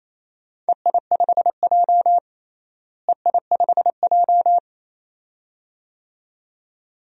Morse code, heard as a keyed tone, about 27 wpm, 700 Hz. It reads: EI5J EI5J
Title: DXCC-100 Call Signs - 4 Characters - Repeated 2x - 28wpm